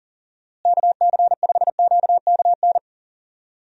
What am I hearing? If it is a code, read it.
Morse code, heard as a keyed tone, about 40 wpm, 700 Hz. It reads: KC5QKN
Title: USA Call Signs - 6 Characters - Repeated 3x - 40wpm